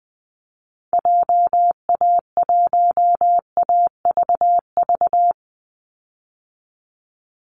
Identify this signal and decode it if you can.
Morse code, heard as a keyed tone, about 20 wpm, 700 Hz. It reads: JA1AVV
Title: USA Call Signs 20wpm